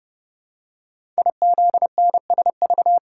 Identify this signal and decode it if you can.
Morse code, heard as a keyed tone, about 28 wpm, 700 Hz. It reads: IZNSV
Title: Code Groups - 5 Letters - 30wpm